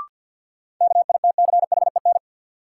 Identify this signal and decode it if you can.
Morse code, heard as a keyed tone, about 53 wpm, 700 Hz. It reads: KITCHEN
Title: Sentences from Top 1400 Words - New Word Drill Alt - 50wpm